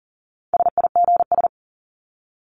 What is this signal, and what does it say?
Morse code, heard as a keyed tone, about 40 wpm, 700 Hz. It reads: SIGS
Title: Abbreviations 40wpm